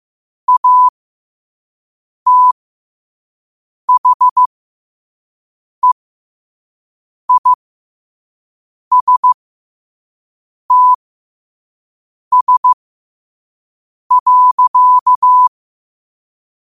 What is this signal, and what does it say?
Morse code, heard as a keyed tone, about 15 wpm, spaced as for 5 wpm, 1000 Hz. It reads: ATHEISTS.
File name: qotdfiles / 5wpm